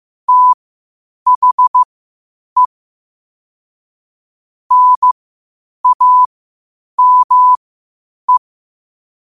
Morse code, heard as a keyed tone, about 15 wpm, spaced as for 8 wpm, 1000 Hz. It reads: THE NAME
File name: qotdfiles / 8wpm